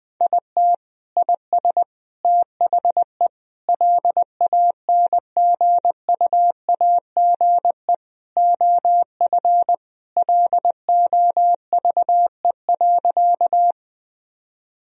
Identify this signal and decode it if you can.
Morse code, heard as a keyed tone, about 20 wpm, 700 Hz. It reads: IT IS THE LANGUAGE OF LOVE.
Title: Sentences from Top 1000 Words - Review Part 3 - 20wpm